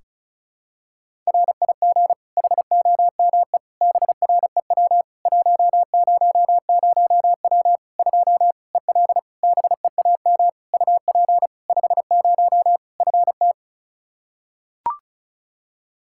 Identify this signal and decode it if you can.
Morse code, heard as a keyed tone, about 36 wpm, 700 Hz. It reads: RIG HOME BREW 100W 2 EL BEAM UP 50 FT
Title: Practice QSO - Exchange 2 - Info - 35wpm